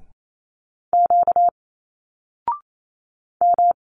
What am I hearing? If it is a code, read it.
Morse code, heard as a keyed tone, about 28 wpm, 700 Hz. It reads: Q M
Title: Single Letters 28wpm